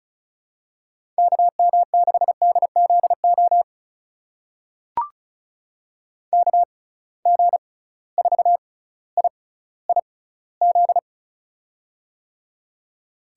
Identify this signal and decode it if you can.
Morse code, heard as a keyed tone, about 36 wpm, spaced as for 18 wpm, 700 Hz. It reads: KM6DZO KG4IIZ
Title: USA Call Signs - 6 Characters - 6x Character Spacing - 35wpm